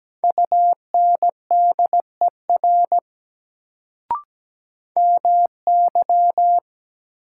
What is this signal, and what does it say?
Morse code, heard as a keyed tone, about 17 wpm, 700 Hz. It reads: UNDER MY
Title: Sets of 2 Words - New Word Drill - 17wpm